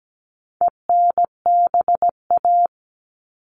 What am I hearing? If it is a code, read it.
Morse code, heard as a keyed tone, about 17 wpm, 700 Hz. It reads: ENBA
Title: All Cut Numbers - 4-Digit 17wpm